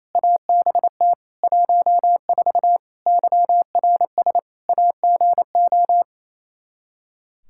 Morse code, heard as a keyed tone, about 28 wpm, 700 Hz. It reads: ABT 14 YRS AGO
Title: K1USN SST QSO Practice 28wpm